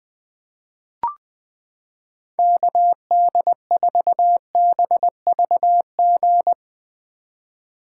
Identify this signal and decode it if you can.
Morse code, heard as a keyed tone, about 20 wpm, 700 Hz. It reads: KD4BVG
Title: 2023 POTA Activator Call Signs - 20wpm